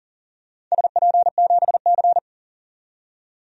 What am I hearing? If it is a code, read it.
Morse code, heard as a keyed tone, about 40 wpm, 700 Hz. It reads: SP7C
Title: DXCC-100 Call Signs - 4 Characters - 40wpm